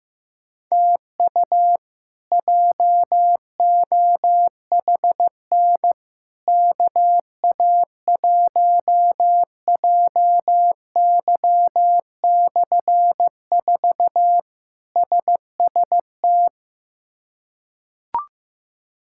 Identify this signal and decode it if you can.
Morse code, heard as a keyed tone, about 16 wpm, 700 Hz. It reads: TU JOHN KA1JY/4 SST
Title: K1USN SST QSO Practice - 15wpm